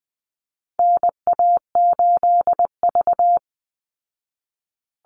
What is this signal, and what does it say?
Morse code, heard as a keyed tone, about 20 wpm, 700 Hz. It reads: NA8V
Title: Call Signs Repeated 3x - 20wpm